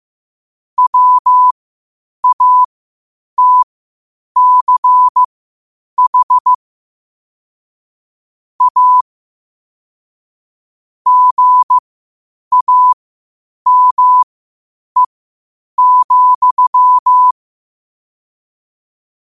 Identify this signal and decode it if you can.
Morse code, heard as a keyed tone, about 15 wpm, spaced as for 8 wpm, 1000 Hz. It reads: WATCH A GAME,
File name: qotdfiles / 8wpm